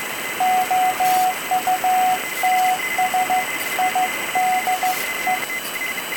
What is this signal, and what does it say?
Morse code, heard as a keyed tone, about 15 wpm, 740 Hz, with noise. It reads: OUTSIDE